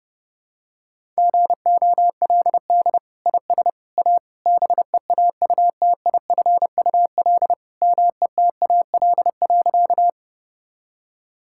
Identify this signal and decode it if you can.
Morse code, heard as a keyed tone, about 30 wpm, 700 Hz. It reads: GOLD IS A BEAUTIFUL METAL.
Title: Sentences from Top 1000 Words - Review Part 6 - 30wpm